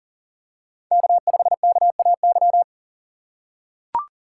Morse code, heard as a keyed tone, about 40 wpm, 700 Hz. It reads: K5KAY
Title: USA Call Signs - 5 Characters - 3x Character Spacing - 40wpm